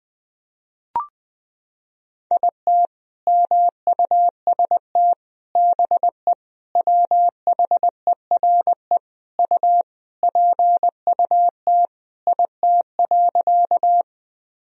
Morse code, heard as a keyed tone, about 20 wpm, 700 Hz. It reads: IT MUST BE WHERE U PUT IT.
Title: Sentences from Top 200 Words and Abbreviations 20wpm